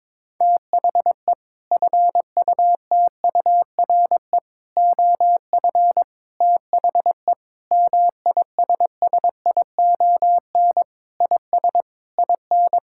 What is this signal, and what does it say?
Morse code, heard as a keyed tone, about 22 wpm, 700 Hz. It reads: THE FUTURE OF THE MISSION IS IN
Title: Sentences from Top 1600 Words 22wpm